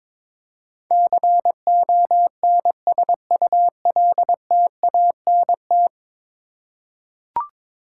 Morse code, heard as a keyed tone, about 22 wpm, 700 Hz. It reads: CONSULTANT
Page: morse-code-ninja-practice-22wpm